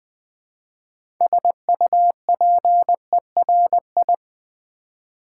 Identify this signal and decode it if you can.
Morse code, heard as a keyed tone, about 20 wpm, 700 Hz. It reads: SUPERI
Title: Build A Word - 2 Letters At A Time - 8-Letter Words - 20wpm